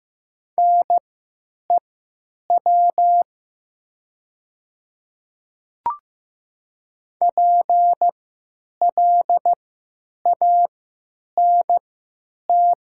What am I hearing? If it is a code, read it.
Morse code, heard as a keyed tone, about 15 wpm, spaced as for 9 wpm, 700 Hz. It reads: NEW PLANT
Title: 3x Character Spacing - Top 300 Words - 15wpm